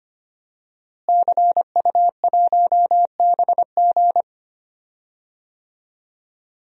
Morse code, heard as a keyed tone, about 25 wpm, 700 Hz. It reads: CU1BG
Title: DXCC-100 Call Signs - 5 Characters - Repeated 2x - 25wpm